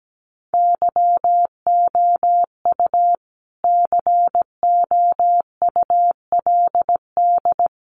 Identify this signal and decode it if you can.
Morse code, heard as a keyed tone, about 17 wpm, 700 Hz. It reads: YOU COULD
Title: Sentences from Top 1500 Words 17wpm